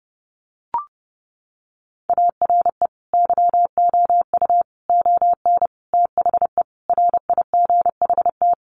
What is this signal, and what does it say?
Morse code, heard as a keyed tone, about 30 wpm, 700 Hz. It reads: ARE YOU ON THE RIGHT
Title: Sentences from Top 1100 Words 30wpm